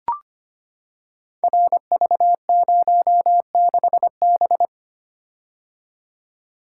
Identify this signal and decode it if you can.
Morse code, heard as a keyed tone, about 25 wpm, 700 Hz. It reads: RV06B
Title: Code Groups - 5 Letter:Numbers - 25wpm